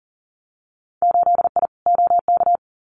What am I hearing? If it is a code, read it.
Morse code, heard as a keyed tone, about 40 wpm, 700 Hz. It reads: 8I OK
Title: Winter Field Day Exchanges Encore 40wpm